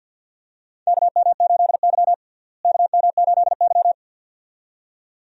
Morse code, heard as a keyed tone, about 52 wpm, 700 Hz. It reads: KM8Y KM8Y
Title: USA Call Signs - 4 Characters - Repeated 2x - 50wpm